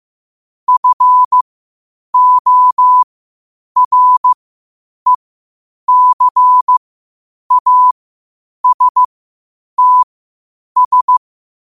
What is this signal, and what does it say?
Morse code, heard as a keyed tone, about 15 wpm, spaced as for 8 wpm, 1000 Hz. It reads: FORECASTS
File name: qotdfiles / 8wpm